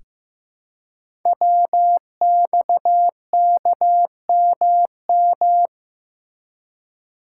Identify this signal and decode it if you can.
Morse code, heard as a keyed tone, about 15 wpm, 700 Hz. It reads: WXKMM
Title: Code Groups - 5 Letters - 15wpm